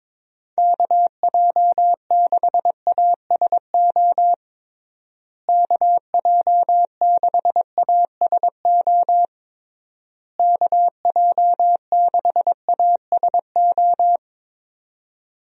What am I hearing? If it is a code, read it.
Morse code, heard as a keyed tone, about 22 wpm, 700 Hz. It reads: KJ6ASO KJ6ASO KJ6ASO
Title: USA Call Signs - 6 Characters - Repeated 3x - 22wpm